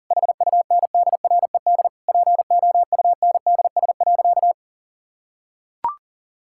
Morse code, heard as a keyed tone, about 40 wpm, 700 Hz. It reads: HUNDRED POUNDS.
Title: Sentences from Top 1000 Words - Review Part 6 - 40wpm